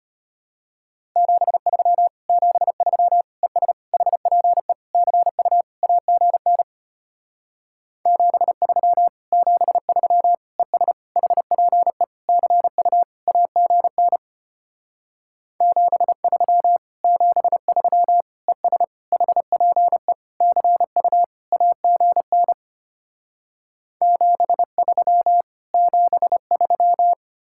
Morse code, heard as a keyed tone, about 33 wpm, 700 Hz. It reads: 73 73 ES HPE CU AGN 73 73 ES HPE CU AGN 73 73 ES HPE CU AGN 73 73